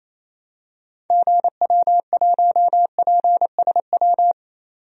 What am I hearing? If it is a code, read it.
Morse code, heard as a keyed tone, about 28 wpm, 700 Hz. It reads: GW1PSW
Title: DXCC-100 Call Signs - 6 Characters - 28wpm